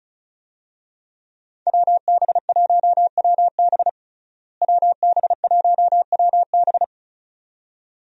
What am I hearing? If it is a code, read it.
Morse code, heard as a keyed tone, about 36 wpm, 700 Hz. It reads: WB1WB WB1WB
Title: USA Call Signs - 5 Characters - Repeated 2x - 35wpm